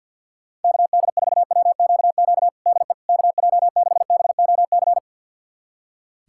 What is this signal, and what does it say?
Morse code, heard as a keyed tone, about 47 wpm, 700 Hz. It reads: KD4WQQ DE KJ6BOC
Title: USA Callsign DE Callsign - 6 Characters - 50wpm